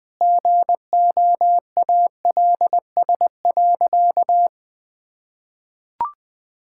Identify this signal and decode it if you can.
Morse code, heard as a keyed tone, about 20 wpm, 700 Hz. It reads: GOALS.
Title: Difficult Sentences with Hints - 20wpm